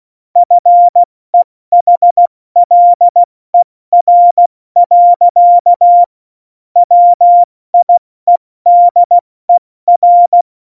Morse code, heard as a keyed tone, about 16 wpm, 700 Hz. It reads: FEHLER. WIEDER
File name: Morsekurs / Witze / Witz 29. Januar 2019, 70 F80